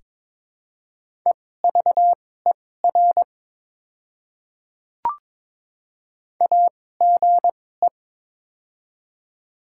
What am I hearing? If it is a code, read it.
Morse code, heard as a keyed tone, about 22 wpm, spaced as for 16 wpm, 700 Hz. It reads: EVER AGE
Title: 2x Character Spacing - Top 500 Words - 22wpm